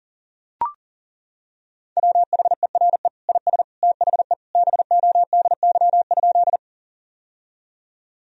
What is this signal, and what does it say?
Morse code, heard as a keyed tone, about 40 wpm, 700 Hz. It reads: WHERE IS THE BODY?
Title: Sentences from Top 1000 Words - Review Part 2 - 40wpm